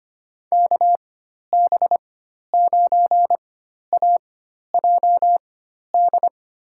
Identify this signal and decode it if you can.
Morse code, heard as a keyed tone, about 25 wpm, spaced as for 12 wpm, 700 Hz. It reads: KB9AJD
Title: USA Call Signs - 4x Character Spacing - 25wpm